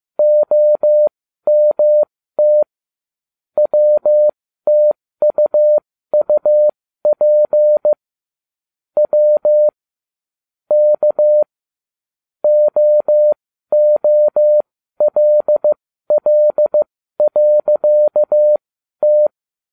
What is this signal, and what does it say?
Morse code, heard as a keyed tone, about 15 wpm, spaced as for 12 wpm, 600 Hz. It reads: OMT WTUUP W K OOLL.T